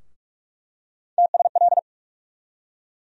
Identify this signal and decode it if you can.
Morse code, heard as a keyed tone, about 45 wpm, 700 Hz. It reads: TSL
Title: USA Call Sign Postfixes - 3 Letters 45wpm